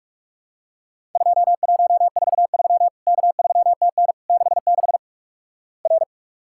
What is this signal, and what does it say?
Morse code, heard as a keyed tone, about 48 wpm, 700 Hz. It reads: 2143 K3TN 66 R
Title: Worked All Europe DX Contest - QTC Exchange - 45wpm